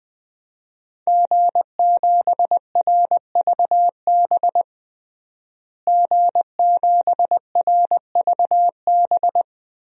Morse code, heard as a keyed tone, about 21 wpm, 700 Hz. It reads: G7RVB G7RVB
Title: DXCC-100 Call Signs - 5 Characters - Repeated 2x - 20wpm